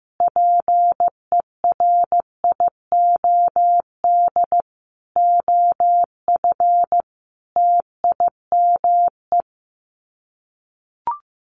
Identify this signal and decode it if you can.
Morse code, heard as a keyed tone, about 15 wpm, 700 Hz. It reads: PERIOD OF TIME
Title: Sets of 5 Words Encore 15wpm